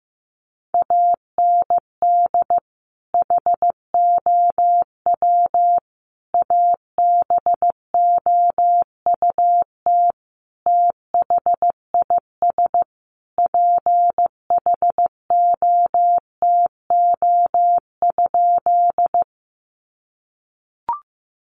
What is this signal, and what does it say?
Morse code, heard as a keyed tone, about 15 wpm, 700 Hz. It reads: AND HOW ABOUT THIS PHOTO?
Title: Sentences from Top 1200 Words 15wpm